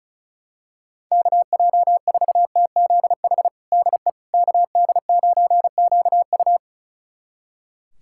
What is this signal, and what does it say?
Morse code, heard as a keyed tone, about 34 wpm, 700 Hz. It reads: KJ4TZH DE KD9QU
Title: USA Callsign DE Callsign 35wpm